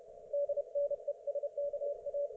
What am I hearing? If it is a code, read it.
Morse code, heard as a keyed tone, about 31 wpm, 560 Hz. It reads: DNESKA